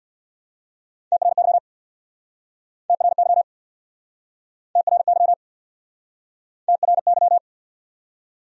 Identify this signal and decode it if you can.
Morse code, heard as a keyed tone, about 68 wpm, 700 Hz. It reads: TRY TRY TRY TRY